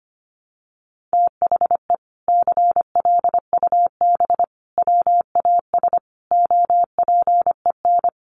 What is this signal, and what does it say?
Morse code, heard as a keyed tone, about 25 wpm, 700 Hz. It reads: THE CLUB WAS OPEN
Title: Sentences from Top 1200 Words 25wpm